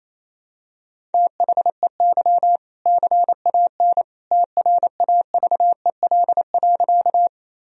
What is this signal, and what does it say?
Morse code, heard as a keyed tone, about 28 wpm, 700 Hz. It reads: THEY CAN TRAVEL.
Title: Sentences from Top 1000 Words - Review Part 2 - 28wpm